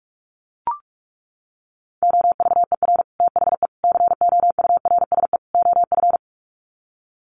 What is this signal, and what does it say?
Morse code, heard as a keyed tone, about 44 wpm, 700 Hz. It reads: OVER THE COURSE OF
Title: Sets of 4 Words 45wpm